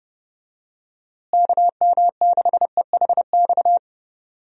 Morse code, heard as a keyed tone, about 30 wpm, 700 Hz. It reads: KM6EHX
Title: USA Call Signs - 6 Characters - Repeated 2x - 30wpm